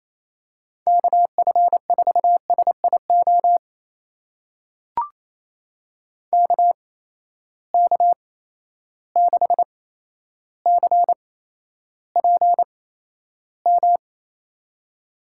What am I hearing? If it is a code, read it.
Morse code, heard as a keyed tone, about 28 wpm, spaced as for 12 wpm, 700 Hz. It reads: KF4SIO KK6CPM
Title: USA Call Signs - 6 Characters - 8x Character Spacing - 28wpm